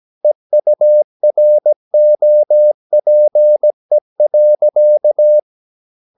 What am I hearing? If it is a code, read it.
Morse code, heard as a keyed tone, about 17 wpm, 600 Hz. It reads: EUROPE.
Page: Romanen Never af Ken Follet på engelsk.